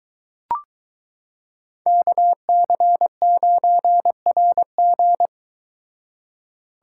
Morse code, heard as a keyed tone, about 22 wpm, 700 Hz. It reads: KC9RG